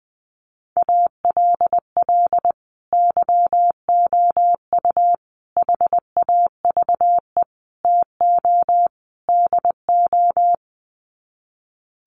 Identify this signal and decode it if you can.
Morse code, heard as a keyed tone, about 20 wpm, 700 Hz. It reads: ALL YOU HAVE TO DO
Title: Sets of 5 Words 20wpm